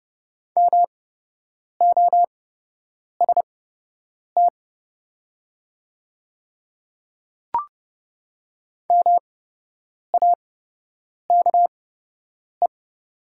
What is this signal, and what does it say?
Morse code, heard as a keyed tone, about 30 wpm, spaced as for 8 wpm, 700 Hz. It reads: MOST MAKE
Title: 8x Character Spacing - Top 100 Words - 30wpm